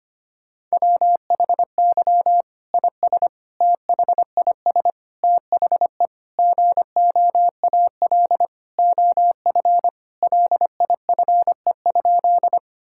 Morse code, heard as a keyed tone, about 25 wpm, 700 Hz. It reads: WHY IS THIS THE GOAL OF LIFE?
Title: Sentences from Top 1000 Words - Review Part 5 - 25wpm